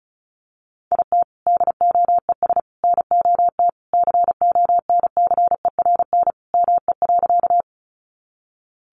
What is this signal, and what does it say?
Morse code, heard as a keyed tone, about 34 wpm, 700 Hz. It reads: IT DOES NOT CONCERN ME.
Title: Sentences from Top 1000 Words 35wpm